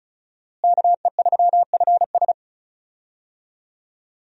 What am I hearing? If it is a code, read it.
Morse code, heard as a keyed tone, about 34 wpm, 700 Hz. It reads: KE3FS
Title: USA Call Signs - 5 Characters - Repeated 2x - 35wpm